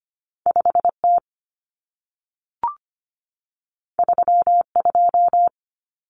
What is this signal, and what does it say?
Morse code, heard as a keyed tone, about 25 wpm, 700 Hz. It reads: 5T 32
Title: Common Cut Numbers - 2-Digit 25wpm